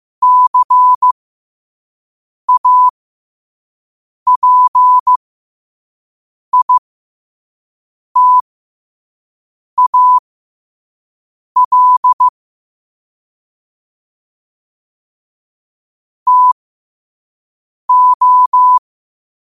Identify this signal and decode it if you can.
Morse code, heard as a keyed tone, about 15 wpm, spaced as for 5 wpm, 1000 Hz. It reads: CAPITAL TO
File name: qotdfiles / 5wpm